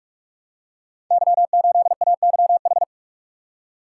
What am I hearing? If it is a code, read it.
Morse code, heard as a keyed tone, about 48 wpm, 700 Hz. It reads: Y8AYH
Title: Code Groups - 5 Letter:Numbers - 45wpm